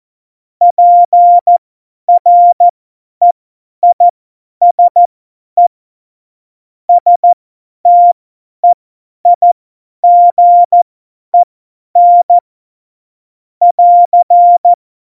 Morse code, heard as a keyed tone, about 14 wpm, spaced as for 10 wpm, 700 Hz. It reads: PREISE STEIGEN <AR>
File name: Morsekurs / Witze / Witz 25 Juli 17, Tempo 50 F70